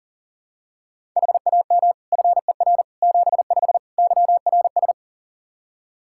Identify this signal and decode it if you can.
Morse code, heard as a keyed tone, about 40 wpm, 700 Hz. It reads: HAM FER 75 YRS
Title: K1USN SST QSO Practice 40wpm